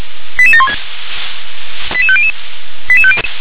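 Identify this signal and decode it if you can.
Five-tone selective call (ZVEI), heard as quick runs of five tones, tones of about 70 ms. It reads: 88716 88500
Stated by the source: ZVEI five-tone call received by SDR.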